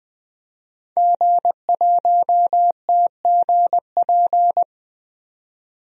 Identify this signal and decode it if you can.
Morse code, heard as a keyed tone, about 20 wpm, 700 Hz. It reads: G1TGP
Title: DXCC-100 Call Signs - 5 Characters - 20wpm